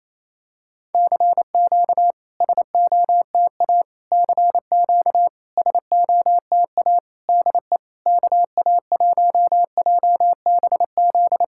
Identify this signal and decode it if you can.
Morse code, heard as a keyed tone, about 28 wpm, 690 Hz. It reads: CQ SOTA CQ SOTA DE KA1JBZ
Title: SOTA QSO Practice 28wpm